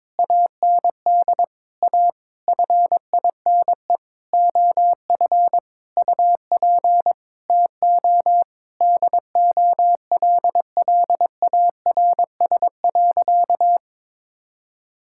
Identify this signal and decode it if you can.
Morse code, heard as a keyed tone, about 22 wpm, 700 Hz. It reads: AND A FINE OF UP TO DOLLARS.
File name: Sentences with 9-Word Phrases - 22wpm